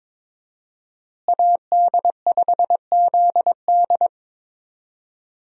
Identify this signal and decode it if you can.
Morse code, heard as a keyed tone, about 22 wpm, 700 Hz. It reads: AD5ZD
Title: USA Call Signs - 5 Characters - Repeated 3x - 22wpm